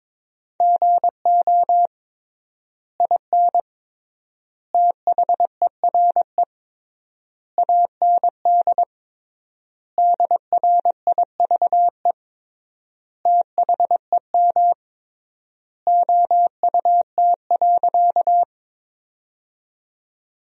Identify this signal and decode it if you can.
Morse code, heard as a keyed tone, about 21 wpm, 700 Hz. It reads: GO IN THERE AND DRIVE THEM OUT.
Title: 3x Word-Spacing - Sentences from Top 500 Words - 22wpm